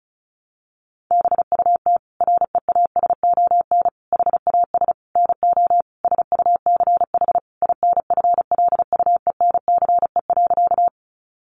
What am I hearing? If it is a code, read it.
Morse code, heard as a keyed tone, about 34 wpm, 700 Hz. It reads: BUT REASON HAS NO SUCH INFLUENCE.
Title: Sentences from Top 1100 Words 35wpm